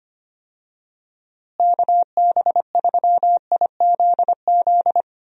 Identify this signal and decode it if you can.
Morse code, heard as a keyed tone, about 25 wpm, 700 Hz. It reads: KB3IZZ
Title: USA Call Signs - 6 Characters - Repeated 2x - 25wpm